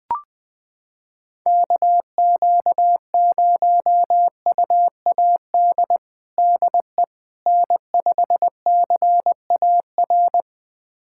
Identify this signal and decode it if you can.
Morse code, heard as a keyed tone, about 20 wpm, 700 Hz. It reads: KQ0UAD DE N5CAR
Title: USA Callsign DE Callsign 20wpm